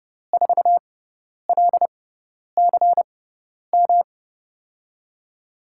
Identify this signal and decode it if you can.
Morse code, heard as a keyed tone, about 32 wpm, spaced as for 10 wpm, 700 Hz. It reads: 4LCM
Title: USA Call Signs - 6 Characters - 6x Character Spacing - 30wpm